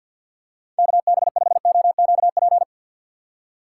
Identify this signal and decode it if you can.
Morse code, heard as a keyed tone, about 53 wpm, 700 Hz. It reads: KB5OQP
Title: USA Call Signs - 6 Characters - 3x Character Spacing - 50wpm